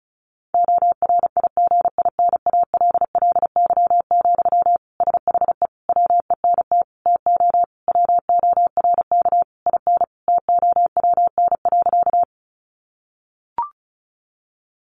Morse code, heard as a keyed tone, about 34 wpm, 700 Hz. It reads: ORIGINALLY, SHE WENT TO WORK IN TOWN.
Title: Sentences from Top 1500 Words 35wpm